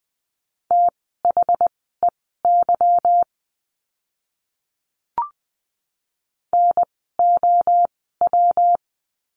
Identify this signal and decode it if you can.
Morse code, heard as a keyed tone, about 20 wpm, spaced as for 14 wpm, 700 Hz. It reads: THEY NOW
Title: 192 Lesson 31f - Letter v - Words 20wpm